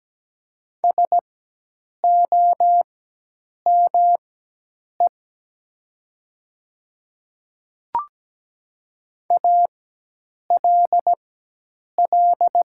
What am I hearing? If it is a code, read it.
Morse code, heard as a keyed tone, about 17 wpm, spaced as for 8 wpm, 700 Hz. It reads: SOME ALL
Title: 4x Character Spacing - Top 100 Words - 17wpm